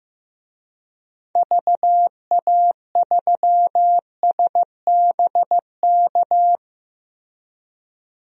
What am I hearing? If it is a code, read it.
Morse code, heard as a keyed tone, about 15 wpm, 700 Hz. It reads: VA3SBK
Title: DXCC-100 Call Signs - 6 Characters - Repeated 2x - 15wpm